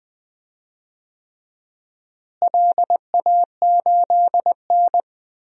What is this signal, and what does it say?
Morse code, heard as a keyed tone, about 21 wpm, 700 Hz. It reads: LA8N
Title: DXCC-100 Call Signs - 4 Characters - Repeated 3x - 20wpm